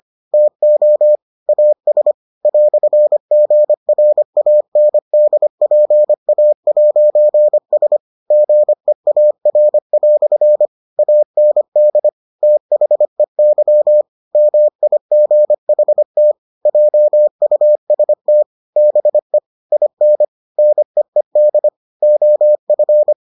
Morse code, heard as a keyed tone, about 25 wpm, 600 Hz. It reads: TO AS "GRANDPA'S GEAR" AND THEY MIGHT JUST BE IN NEED OF